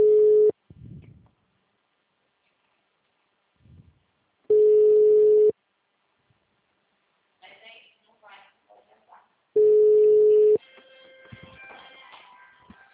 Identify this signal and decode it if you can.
Ringback tone (Europe, 425 Hz), with speech.
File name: out- -2003- - - .53